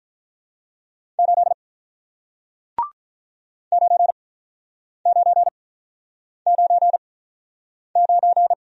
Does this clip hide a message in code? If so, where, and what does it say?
Yes: Morse code from 1.2 to 8.5 s, heard as a keyed tone, about 47 wpm, spaced as for 9 wpm, 700 Hz. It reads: 8 9999